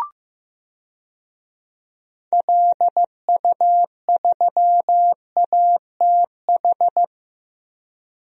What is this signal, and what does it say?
Morse code, heard as a keyed tone, about 15 wpm, 700 Hz. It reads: LU3ATH
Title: DXCC-100 Call Signs - 6 Characters - 15wpm